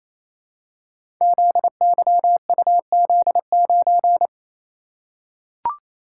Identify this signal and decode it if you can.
Morse code, heard as a keyed tone, about 27 wpm, 700 Hz. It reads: ZYUZ9
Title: Code Groups - 5 Letter:Numbers - 28wpm